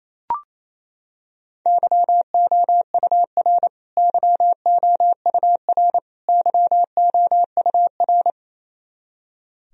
Morse code, heard as a keyed tone, about 28 wpm, 700 Hz. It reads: YOUR YOUR YOUR
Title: Top 100 Words Repeated 28wpm